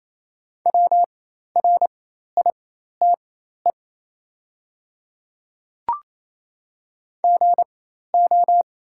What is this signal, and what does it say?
Morse code, heard as a keyed tone, about 28 wpm, spaced as for 13 wpm, 690 Hz. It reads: WRITE GO
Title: 4x Character Spacing - Top 100 Words - 28wpm